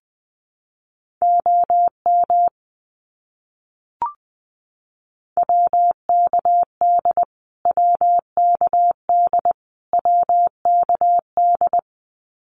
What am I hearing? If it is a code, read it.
Morse code, heard as a keyed tone, about 19 wpm, 700 Hz. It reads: OM WKD WKD WKD
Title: Abbreviations Repeated 3x - 20wpm